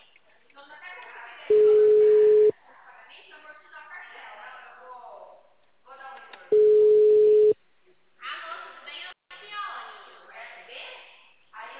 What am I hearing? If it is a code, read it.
Ringback tone (Europe, 425 Hz), with speech.
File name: out- -2044- - - .30049